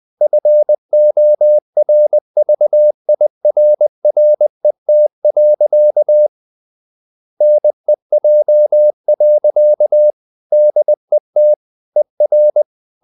Morse code, heard as a keyed tone, about 20 wpm, 600 Hz. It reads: FORVIRRET. NEJ. DET ER
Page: Ferie 20 wpm